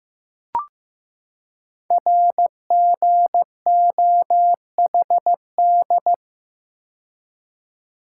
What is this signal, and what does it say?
Morse code, heard as a keyed tone, about 15 wpm, 700 Hz. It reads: RGOHD
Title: Code Groups - 5 Letters - 15wpm